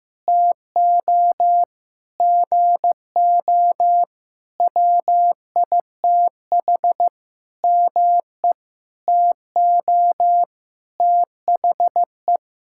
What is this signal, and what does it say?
Morse code, heard as a keyed tone, about 15 wpm, 700 Hz. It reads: TO GO WITH ME TO THE
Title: Sentences from Top 1000 Words - Review Part 6 - 15wpm